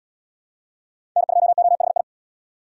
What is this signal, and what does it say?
Morse code, heard as a keyed tone, about 71 wpm, 700 Hz. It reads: N2OHI